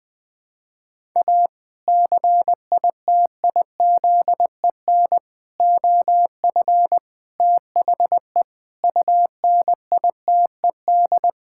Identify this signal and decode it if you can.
Morse code, heard as a keyed tone, about 20 wpm, 700 Hz. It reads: A CITIZEN OF THE UNITED
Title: Sentences with 3-Word Phrases - Spoken Beginning - 20wpm